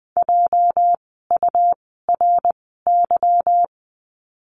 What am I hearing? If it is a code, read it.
Morse code, heard as a keyed tone, about 20 wpm, spaced as for 15 wpm, 700 Hz. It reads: JURY
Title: 206 Lesson 33f - Letter j - Words 20wpm